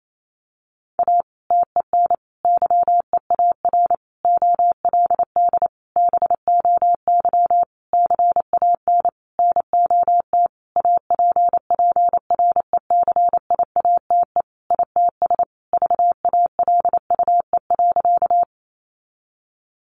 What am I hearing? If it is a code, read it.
Morse code, heard as a keyed tone, about 28 wpm, 700 Hz. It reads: A TEN YEAR OLD BOY CAN NOT APPRECIATE ITS VALUE.
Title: Sentences from Top 1300 Words 28wpm